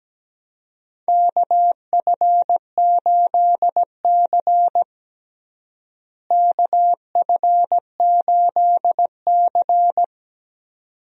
Morse code, heard as a keyed tone, about 17 wpm, 700 Hz. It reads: KF8C KF8C
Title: USA Call Signs - 4 Characters - Repeated 2x - 17wpm